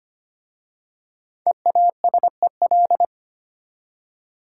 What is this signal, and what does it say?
Morse code, heard as a keyed tone, about 25 wpm, 700 Hz. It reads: EASEL
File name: practice / 25wpm / Easy Words with Hints - 25wpm